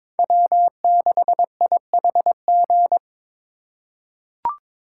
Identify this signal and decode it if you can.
Morse code, heard as a keyed tone, about 22 wpm, 700 Hz. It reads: W6IHG
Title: K1USN SST QSO Practice - 22wpm